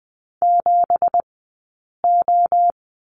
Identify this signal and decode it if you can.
Morse code, heard as a keyed tone, about 20 wpm, spaced as for 8 wpm, 700 Hz. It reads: 7O
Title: 171 Lesson 28d - Number 9 - Two Characters 20wpm